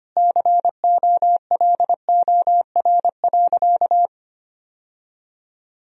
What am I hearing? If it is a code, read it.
Morse code, heard as a keyed tone, about 25 wpm, 700 Hz. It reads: COLOR.
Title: Sentences from Top 1000 Words - Review Part 2 - 25wpm